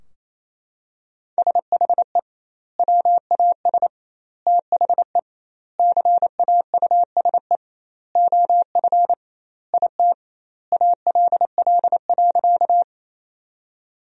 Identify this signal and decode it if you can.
Morse code, heard as a keyed tone, about 28 wpm, 700 Hz. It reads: SHE WAS THE CAUSE OF IT ALL.
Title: 2x Word-Spacing - Sentences from Top 200 Words - 28wpm